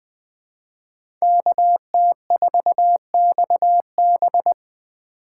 Morse code, heard as a keyed tone, about 20 wpm, 700 Hz. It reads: KT4XB
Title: Speed Racing - USA 2x2 Call Signs 17wpm